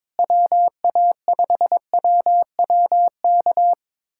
Morse code, heard as a keyed tone, about 21 wpm, 700 Hz. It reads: WA5WWK
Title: USA Callsign DE Callsign - 6 Characters - 22wpm